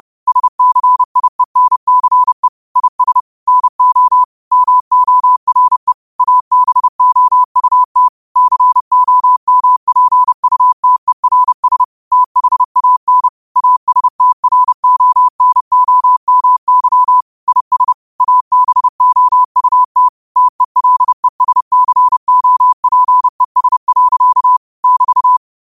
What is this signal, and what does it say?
Morse code, heard as a keyed tone, about 30 wpm, 1000 Hz. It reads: SCIENCE IS NO MORE ABOUT COMPUTERS THAN ASTRONOMY IS ABOUT TELESCOPES. <BT>